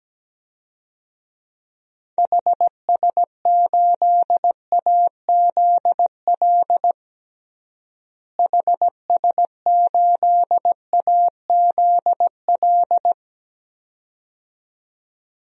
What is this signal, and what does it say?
Morse code, heard as a keyed tone, about 17 wpm, 700 Hz. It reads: HS8AZL HS8AZL
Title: DXCC-100 Call Signs - 6 Characters - Repeated 2x - 17wpm